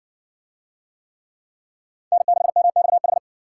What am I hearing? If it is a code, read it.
Morse code, heard as a keyed tone, about 65 wpm, 700 Hz. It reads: N6MXH